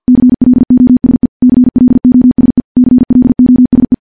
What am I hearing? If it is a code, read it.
Morse code, heard as a keyed tone, about 50 wpm, 250 Hz. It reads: CLOSE CLOSE CLOSE
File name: morse / morse-training-50WPM